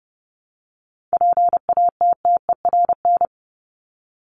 Morse code, heard as a keyed tone, about 30 wpm, 700 Hz. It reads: PATTERN
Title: Top 500 Words Repeated 30wpm